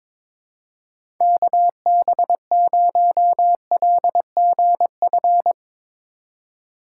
Morse code, heard as a keyed tone, about 22 wpm, 700 Hz. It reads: KB0LGF
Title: K1USN SST QSO Practice - 22wpm